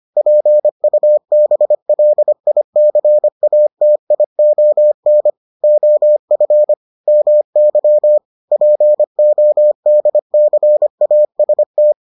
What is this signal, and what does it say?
Morse code, heard as a keyed tone, about 25 wpm, 600 Hz. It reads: PUBLICATION OF MY PODCAST